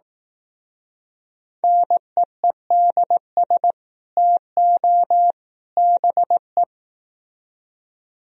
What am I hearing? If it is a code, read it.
Morse code, heard as a keyed tone, about 18 wpm, 700 Hz. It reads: NEEDS TO BE
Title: Speed-Racing - Sets of 3 Words 15wpm